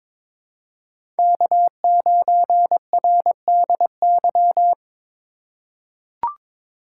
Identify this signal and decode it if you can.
Morse code, heard as a keyed tone, about 22 wpm, 700 Hz. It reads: K9RDY
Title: USA Call Signs - 2x Character Spacing - 22wpm